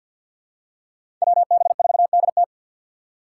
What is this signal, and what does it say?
Morse code, heard as a keyed tone, about 51 wpm, 700 Hz. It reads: WB4DT
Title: USA Call Signs - 3x Character Spacing - 50wpm